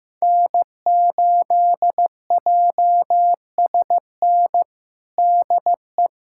Morse code, heard as a keyed tone, about 16 wpm, 700 Hz. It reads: N8JSN DE
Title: USA Callsign DE Callsign 15wpm